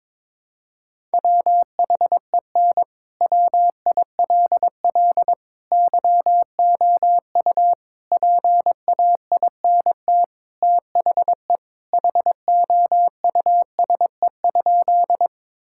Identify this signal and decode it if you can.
Morse code, heard as a keyed tone, about 22 wpm, 700 Hz. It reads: WHEN WILL YOU PAINT THE HOUSE?
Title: Sentences from Top 1000 Words - Review Part 6 - 22wpm